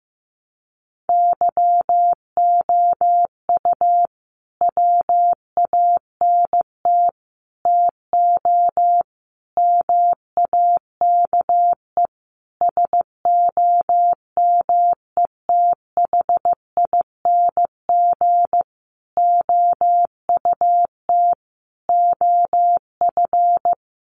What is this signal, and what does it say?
Morse code, heard as a keyed tone, about 15 wpm, 700 Hz. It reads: YOU WANT TO MAKE SOMETHING OUT OF
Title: Sentences from Top 600 Words 15wpm